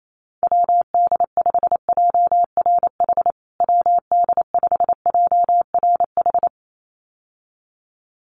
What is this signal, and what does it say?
Morse code, heard as a keyed tone, about 28 wpm, 700 Hz. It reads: WD5JRH WD5JRH
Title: USA Call Signs Repeated 2x 28wpm